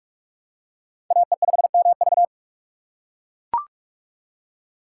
Morse code, heard as a keyed tone, about 40 wpm, 700 Hz. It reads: AE5MV